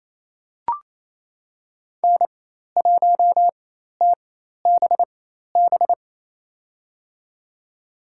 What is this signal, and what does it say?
Morse code, heard as a keyed tone, about 28 wpm, spaced as for 13 wpm, 700 Hz. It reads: N1TBB
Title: USA Call Signs - 4x Character Spacing - 28wpm